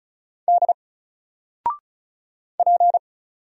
Morse code, heard as a keyed tone, about 36 wpm, 700 Hz. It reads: D P
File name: Single-Letter-Number-35wpm